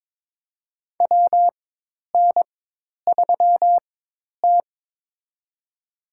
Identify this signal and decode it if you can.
Morse code, heard as a keyed tone, about 23 wpm, spaced as for 10 wpm, 700 Hz. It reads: WN3T
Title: USA Call Signs - 4 Characters - 4x Character Spacing - 22wpm